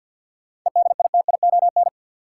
Morse code, heard as a keyed tone, about 49 wpm, 700 Hz. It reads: EDITION
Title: Sets of 5 Words - New Word Drill - 50wpm